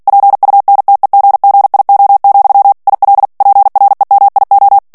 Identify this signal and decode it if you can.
Morse code, heard as a keyed tone, about 48 wpm, 800 Hz. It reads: PUNTEGGIO, IL PREMIO
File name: ITALIEN texte clair 2 vitesse 45 mots minute